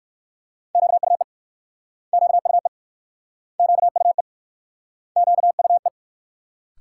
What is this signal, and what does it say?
Morse code, heard as a keyed tone, about 60 wpm, 690 Hz. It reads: QUE QUE QUE QUE